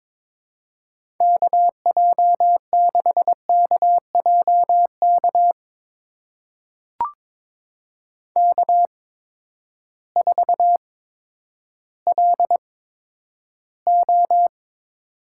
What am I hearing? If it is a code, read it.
Morse code, heard as a keyed tone, about 22 wpm, spaced as for 11 wpm, 700 Hz. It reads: KJ6KJK K4LO
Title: USA Call Signs - 8x Character Spacing - 22wpm